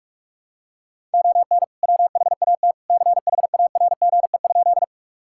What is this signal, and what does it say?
Morse code, heard as a keyed tone, about 41 wpm, 690 Hz. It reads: ON WHAT CHARGE?
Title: Sentences from Top 1000 Words - Review Part 6 - 45wpm